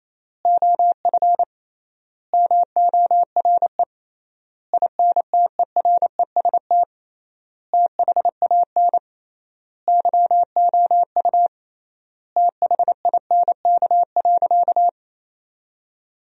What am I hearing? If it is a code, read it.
Morse code, heard as a keyed tone, about 28 wpm, 700 Hz. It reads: OF MORE INTEREST THAN YOU THINK.
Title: 3x Word-Spacing - Sentences from Top 400 Words - 28wpm